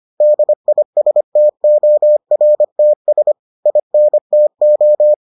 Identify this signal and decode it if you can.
Morse code, heard as a keyed tone, about 25 wpm, 600 Hz. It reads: DISTORTS INTO